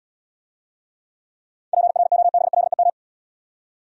Morse code, heard as a keyed tone, about 80 wpm, 700 Hz. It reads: PROBLEM